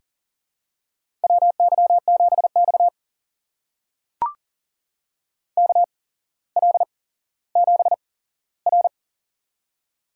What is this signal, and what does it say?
Morse code, heard as a keyed tone, about 40 wpm, spaced as for 17 wpm, 700 Hz. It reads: WY7X KL7R
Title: USA Call Signs - 4 Characters - 8x Character Spacing - 40wpm